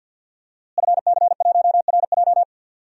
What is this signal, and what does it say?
Morse code, heard as a keyed tone, about 52 wpm, 700 Hz. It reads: FC1RJ
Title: DXCC-100 Call Signs - 5 Characters - Repeated 2x - 50wpm